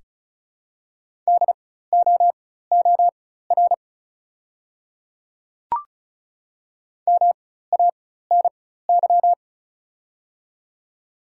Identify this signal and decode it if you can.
Morse code, heard as a keyed tone, about 36 wpm, spaced as for 16 wpm, 690 Hz. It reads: DOOR MANY
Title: 4x Character Spacing - Top 300 Words - 35wpm